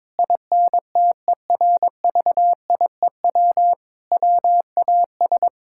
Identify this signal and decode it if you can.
Morse code, heard as a keyed tone, about 22 wpm, 700 Hz. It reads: INTERVIEW WAS
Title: Sentences from Top 1600 Words 22wpm